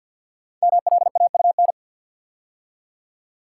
Morse code, heard as a keyed tone, about 51 wpm, 700 Hz. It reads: MLAUN
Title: Code Groups - 5 Letters - 50wpm